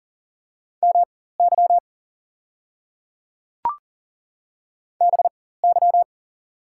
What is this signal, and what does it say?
Morse code, heard as a keyed tone, about 40 wpm, spaced as for 19 wpm, 700 Hz. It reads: MY BY
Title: 4x Character Spacing - Top 200 Words - 40wpm